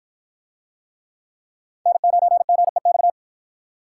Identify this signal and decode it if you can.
Morse code, heard as a keyed tone, about 46 wpm, 700 Hz. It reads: N9GEX